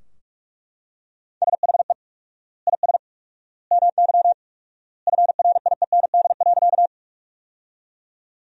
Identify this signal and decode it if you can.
Morse code, heard as a keyed tone, about 42 wpm, 700 Hz. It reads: SHE IS MY FRIEND.
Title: 4x Word-Spacing - Sentences from Top 300 Words - 45wpm